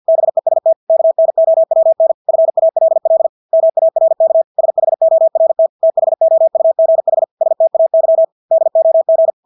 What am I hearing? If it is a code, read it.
Morse code, heard as a keyed tone, about 50 wpm, 650 Hz. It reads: BEST KNOWN FALL MARK SHORT THOUGH STAY DOG
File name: Engelske ord 50 wpm